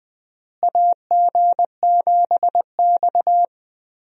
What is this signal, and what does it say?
Morse code, heard as a keyed tone, about 21 wpm, 700 Hz. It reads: AG7X
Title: USA Call Signs - 4 Characters - Repeated 3x - 20wpm